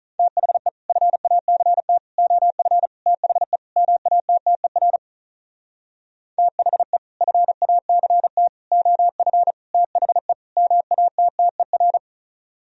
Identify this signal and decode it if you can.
Morse code, heard as a keyed tone, about 43 wpm, 700 Hz. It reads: THE FACT OF THE MATTER THE FACT OF THE MATTER
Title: Speed-Racing - Sets of 5 Words 35wpm